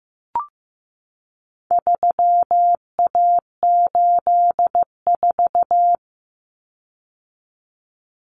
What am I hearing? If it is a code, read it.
Morse code, heard as a keyed tone, about 15 wpm, 700 Hz. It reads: 3A84
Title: Common Cut Numbers - 4-Digit 15wpm